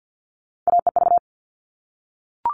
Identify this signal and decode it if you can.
Morse code, heard as a keyed tone, about 48 wpm, 700 Hz. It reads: AEV
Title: All Cut Numbers - 3-Digit 50wpm